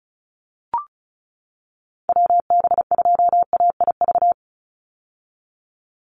Morse code, heard as a keyed tone, about 34 wpm, 700 Hz. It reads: WB2AIV
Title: Call Signs - 35wpm